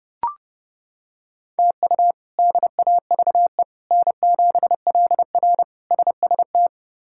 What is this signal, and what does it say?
Morse code, heard as a keyed tone, about 30 wpm, 700 Hz. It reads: TU DAVE N7LR SST
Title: K1USN SST QSO Practice - 30wpm